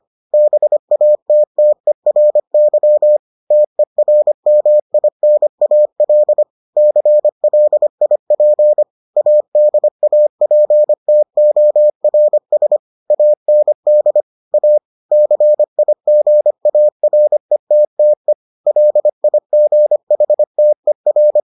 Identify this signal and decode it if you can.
Morse code, heard as a keyed tone, about 25 wpm, 600 Hz. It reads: BATTERY TERMINAL CLIP ADAPTORS AND A CIGARETTE LIGHTER